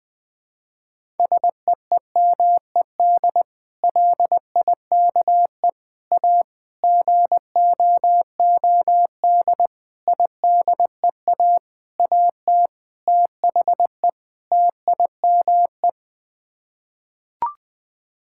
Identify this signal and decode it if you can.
Morse code, heard as a keyed tone, about 20 wpm, 700 Hz. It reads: SEEMED LIKE A GOOD IDEA AT THE TIME
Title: Sets of 8 Words - 20wpm